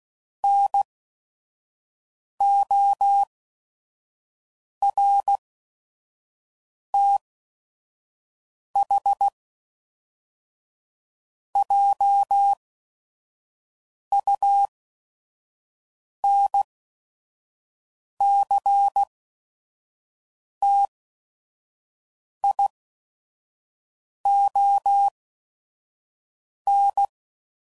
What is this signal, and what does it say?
Morse code, heard as a keyed tone, about 16 wpm, spaced as for 5 wpm, 800 Hz. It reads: NORTHJUNCTION
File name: CW Code Groups @5WPM / fromSM's / Cities